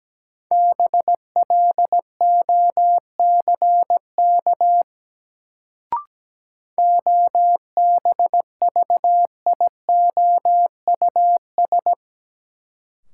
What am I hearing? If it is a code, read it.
Morse code, heard as a keyed tone, about 17 wpm, 700 Hz. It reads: BLOCK OBVIOUS
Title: Sentences from Top 1600 Words - New Word Drill Alt - 17wpm